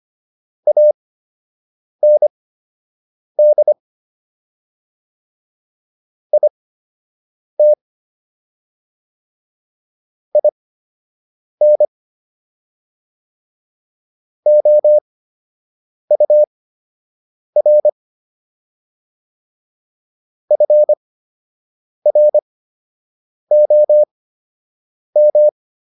Morse code, heard as a keyed tone, about 25 wpm, spaced as for 7 wpm, 600 Hz. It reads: AND IT IN OUR FROM